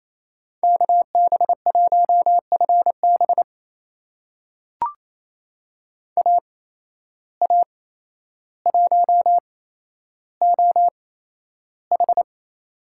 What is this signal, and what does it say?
Morse code, heard as a keyed tone, about 28 wpm, spaced as for 12 wpm, 700 Hz. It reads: KB1FB AA1OH
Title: USA Call Signs - 5 Characters - 8x Character Spacing - 28wpm